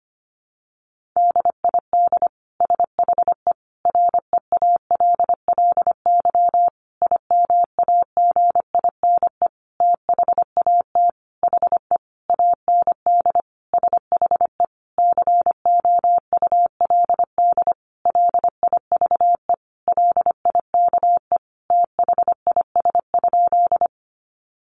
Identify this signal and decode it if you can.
Morse code, heard as a keyed tone, about 25 wpm, 700 Hz. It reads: DID SHE REALLY IMAGINE THAT HE AND SHE COULD LIVE LIKE THIS?
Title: Sentences from Top 1400 Words 25wpm